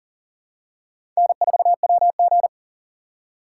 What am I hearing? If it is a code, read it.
Morse code, heard as a keyed tone, about 40 wpm, 700 Hz. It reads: N4WG
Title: USA Call Signs - 4x Character Spacing - 40wpm